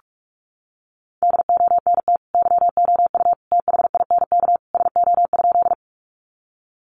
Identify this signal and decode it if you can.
Morse code, heard as a keyed tone, about 44 wpm, 700 Hz. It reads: DONT YOU THINK SO?
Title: Sentences from Top 300 Words 45wpm